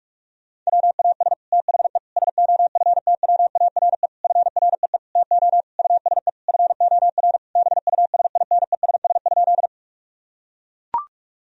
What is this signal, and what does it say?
Morse code, heard as a keyed tone, about 41 wpm, 700 Hz. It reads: WAS THE SOFTWARE FREE TO USE FOR BUSINESS?
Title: Sentences from Top 1000 Words - Review Part 3 - 45wpm